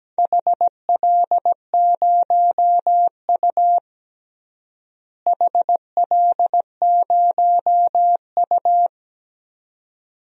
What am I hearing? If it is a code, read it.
Morse code, heard as a keyed tone, about 17 wpm, 700 Hz. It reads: HL0U HL0U
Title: DXCC-100 Call Signs - 4 Characters - Repeated 2x - 17wpm